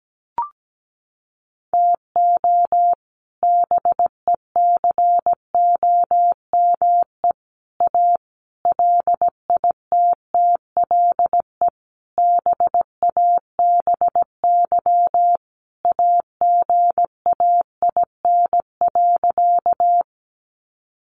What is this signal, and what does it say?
Morse code, heard as a keyed tone, about 17 wpm, 700 Hz. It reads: TO BECOME A LITTLE BABY AGAIN.
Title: Sentences from Top 1300 Words 17wpm